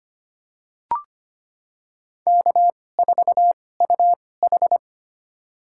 Morse code, heard as a keyed tone, about 25 wpm, spaced as for 18 wpm, 700 Hz. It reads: K4UH
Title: USA Call Signs - 4 Characters - 2x Character Spacing - 25wpm